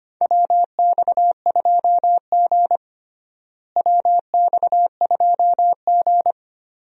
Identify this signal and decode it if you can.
Morse code, heard as a keyed tone, about 25 wpm, 700 Hz. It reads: WX2G WX2G
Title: USA Call Signs - 4 Characters - Repeated 2x - 25wpm